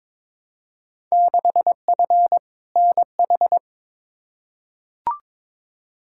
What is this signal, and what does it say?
Morse code, heard as a keyed tone, about 21 wpm, 700 Hz. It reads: 6F NH
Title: Field Day Exchanges Encore - 22wpm